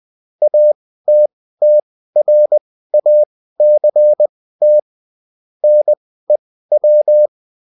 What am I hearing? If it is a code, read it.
Morse code, heard as a keyed tone, about 20 wpm, spaced as for 14 wpm, 600 Hz. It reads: ATTRACT NEW